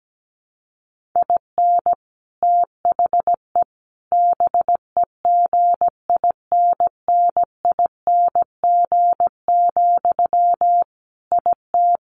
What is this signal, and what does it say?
Morse code, heard as a keyed tone, about 17 wpm, 700 Hz. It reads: IN THE BEGINNING, IT
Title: Sentences from Top 1400 Words 17wpm